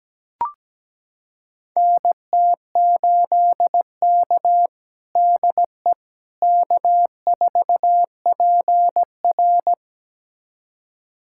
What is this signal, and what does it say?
Morse code, heard as a keyed tone, about 17 wpm, 700 Hz. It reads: NT8K DE K4PR
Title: USA Callsign DE Callsign - 4 Characters - 17wpm